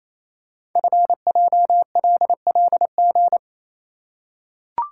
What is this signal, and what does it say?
Morse code, heard as a keyed tone, about 28 wpm, 700 Hz. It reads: FJLLG
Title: Code Groups - 5 Letters - 28wpm